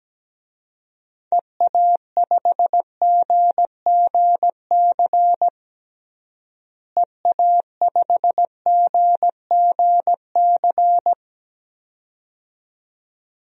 Morse code, heard as a keyed tone, about 17 wpm, 700 Hz. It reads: EA5GGC EA5GGC
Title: DXCC-100 Call Signs - 6 Characters - Repeated 2x - 17wpm